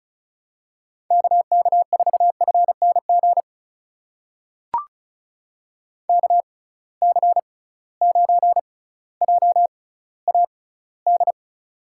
Morse code, heard as a keyed tone, about 36 wpm, spaced as for 18 wpm, 700 Hz. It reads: KK4FNG KC9JAD
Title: USA Call Signs - 6 Characters - 6x Character Spacing - 35wpm